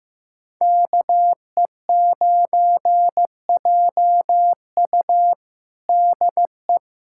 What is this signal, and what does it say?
Morse code, heard as a keyed tone, about 15 wpm, 700 Hz. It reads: KE9JU DE
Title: USA Callsign DE Callsign 15wpm